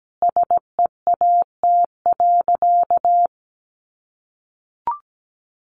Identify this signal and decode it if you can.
Morse code, heard as a keyed tone, about 17 wpm, 700 Hz. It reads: SEAT.
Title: Sentences from Top 1500 Words 17wpm